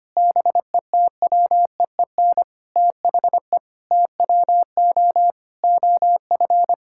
Morse code, heard as a keyed tone, about 25 wpm, 700 Hz. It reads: BETWEEN THE TWO OF
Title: Sentences from Top 1000 Words - Review Part 6 - 25wpm